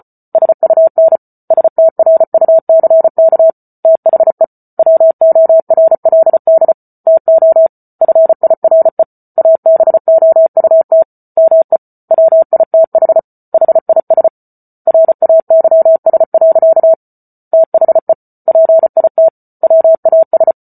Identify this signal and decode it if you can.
Morse code, heard as a keyed tone, about 34 wpm, 650 Hz. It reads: SUN STRUCK THE WORLD TO FIRE ABOUT ME WITH HIS RAYS. THE PIT WAS